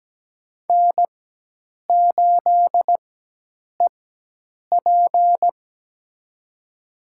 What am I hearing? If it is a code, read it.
Morse code, heard as a keyed tone, about 17 wpm, spaced as for 8 wpm, 700 Hz. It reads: N8EP
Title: USA Call Signs - 4 Characters - 4x Character Spacing - 17wpm